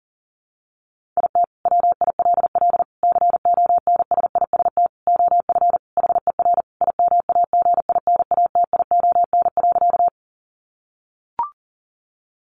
Morse code, heard as a keyed tone, about 37 wpm, 700 Hz. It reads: IT WILL CONSIST OF HER IMAGINATION.
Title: Sentences from Top 1400 Words 40wpm